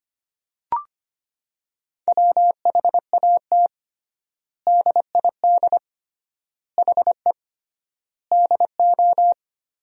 Morse code, heard as a keyed tone, about 25 wpm, 700 Hz. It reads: WHAT DID HE DO
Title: 3x Word-Spacing - Sentences from Top 100 Words - 25wpm